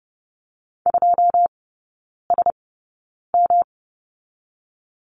Morse code, heard as a keyed tone, about 30 wpm, spaced as for 9 wpm, 700 Hz. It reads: 2SM
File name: Lesson 18e - Letter m - Three Characters 30wpm